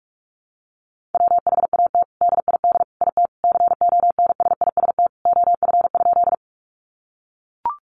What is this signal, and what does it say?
Morse code, heard as a keyed tone, about 44 wpm, 700 Hz. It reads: WHAT DID IT CONSIST OF?
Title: Sentences from Top 1400 Words 45wpm